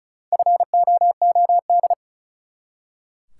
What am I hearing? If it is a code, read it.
Morse code, heard as a keyed tone, about 34 wpm, 700 Hz. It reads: FOOD
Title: Four Letter Words - 35wpm